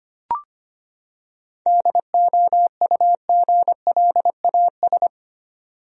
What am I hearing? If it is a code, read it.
Morse code, heard as a keyed tone, about 25 wpm, 700 Hz. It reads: DOUGLAS
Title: USA Common Names - 25wpm